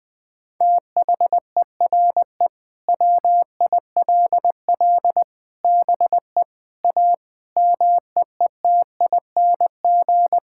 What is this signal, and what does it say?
Morse code, heard as a keyed tone, about 20 wpm, 700 Hz. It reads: THERE WILL BE A MEETING
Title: Sentences with 4-Word Phrases - 20wpm